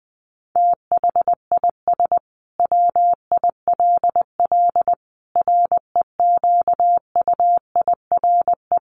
Morse code, heard as a keyed tone, about 20 wpm, 700 Hz. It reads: THIS WILL REQUIRE
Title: Sentences from Top 900 Words 20wpm